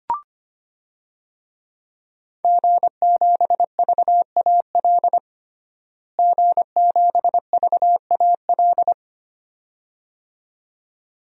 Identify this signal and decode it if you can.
Morse code, heard as a keyed tone, about 25 wpm, 700 Hz. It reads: G7VAL G7VAL
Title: DXCC-100 Call Signs - 5 Characters - Repeated 2x - 25wpm